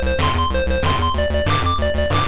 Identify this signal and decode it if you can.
There is no code.